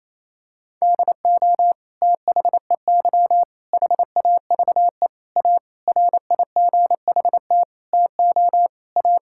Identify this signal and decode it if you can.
Morse code, heard as a keyed tone, about 27 wpm, 700 Hz. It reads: DO THEY HAVE A RIGHT TO A
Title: Sentences from Top 1000 Words - Review Part 6 - 28wpm